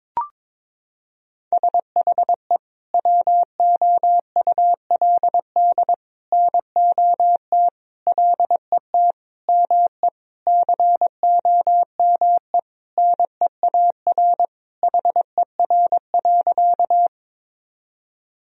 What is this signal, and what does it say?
Morse code, heard as a keyed tone, about 22 wpm, 700 Hz. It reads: SHE WOULD NOT LET ME COME NEAR HER.
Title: Sentences from Top 1000 Words - Review Part 1 - 22wpm